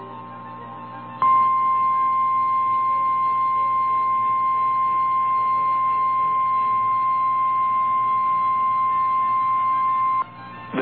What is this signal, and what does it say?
Emergency Alert System (SAME) tones (the 1050 Hz attention tone). It is heard from 1.2 to 10.3 s.
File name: eas / KNCYKBIE / KNCYKBIE